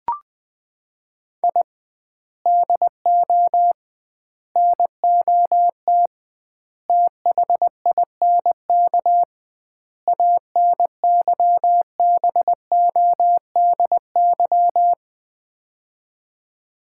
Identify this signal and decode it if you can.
Morse code, heard as a keyed tone, about 20 wpm, 700 Hz. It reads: I DO NOT THINK ANYBODY
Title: 2x Word-Spacing - Sets of 5 Words - 20wpm